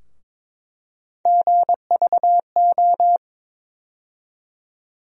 Morse code, heard as a keyed tone, about 22 wpm, 700 Hz. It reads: GVO
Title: USA Call Sign Postfixes - 3 Letters 22wpm